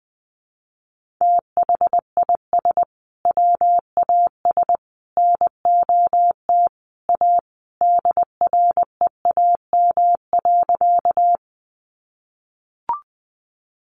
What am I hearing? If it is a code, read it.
Morse code, heard as a keyed tone, about 20 wpm, 700 Hz. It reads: THIS WAS NOT A DREAM.
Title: Sentences from Top 1500 Words 20wpm